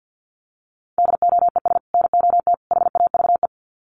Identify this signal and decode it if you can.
Morse code, heard as a keyed tone, about 49 wpm, 700 Hz. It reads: DOES NOT HAVE
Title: Sets of 3 Words 50wpm